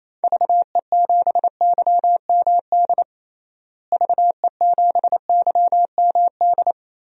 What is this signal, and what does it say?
Morse code, heard as a keyed tone, about 28 wpm, 700 Hz. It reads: VE7YMD VE7YMD
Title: DXCC-100 Call Signs - 6 Characters - Repeated 2x - 28wpm